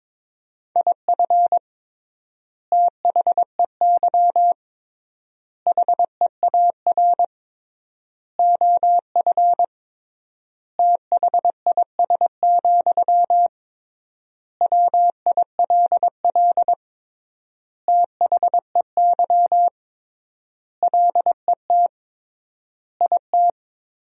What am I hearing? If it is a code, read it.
Morse code, heard as a keyed tone, about 22 wpm, 700 Hz. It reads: IF THEY HEAR OF THIS, WILL THEY LET IT
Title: 3x Word-Spacing - Sentences from Top 400 Words - 22wpm